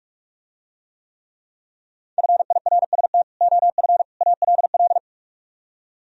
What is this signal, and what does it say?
Morse code, heard as a keyed tone, about 44 wpm, 700 Hz. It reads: FIRST OF ALL